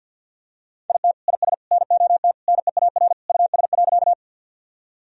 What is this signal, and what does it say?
Morse code, heard as a keyed tone, about 51 wpm, 700 Hz. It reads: IT IS NOT NEAR US.
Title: Sentences from Top 1000 Words - Review Part 1 - 50wpm